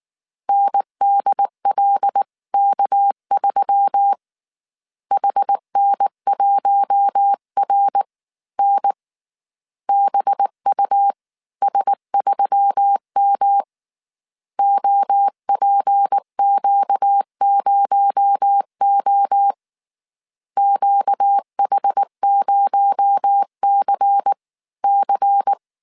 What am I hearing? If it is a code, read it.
Morse code, heard as a keyed tone, about 19 wpm, spaced as for 17 wpm, 800 Hz. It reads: NDLK3 HN1RN BUS3M OPQ0O QH0CC